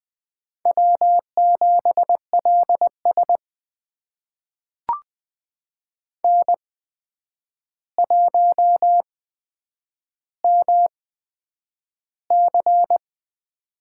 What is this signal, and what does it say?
Morse code, heard as a keyed tone, about 20 wpm, spaced as for 9 wpm, 700 Hz. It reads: W7LS N1MC
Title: USA Call Signs - 4 Characters - 8x Character Spacing - 20wpm